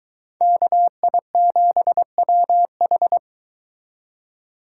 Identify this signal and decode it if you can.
Morse code, heard as a keyed tone, about 23 wpm, 700 Hz. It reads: KI7WH